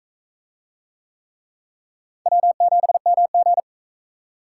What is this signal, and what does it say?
Morse code, heard as a keyed tone, about 37 wpm, 700 Hz. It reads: W7MG